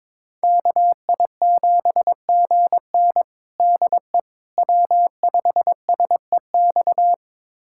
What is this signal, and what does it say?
Morse code, heard as a keyed tone, about 22 wpm, 700 Hz. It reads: KI7GN DE W5SEX
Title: USA Callsign DE Callsign - 5 Characters - 22wpm